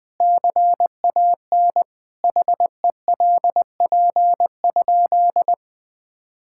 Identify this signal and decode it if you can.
Morse code, heard as a keyed tone, about 20 wpm, 700 Hz. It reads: CAN HELP?
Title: Sentences with 4-Word Phrases - 20wpm